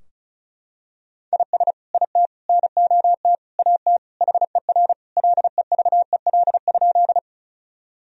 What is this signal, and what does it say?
Morse code, heard as a keyed tone, about 36 wpm, 700 Hz. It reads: IS IT NOT AT HER LEVEL?
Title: Sentences from Top 1000 Words - Review Part 4 - 35wpm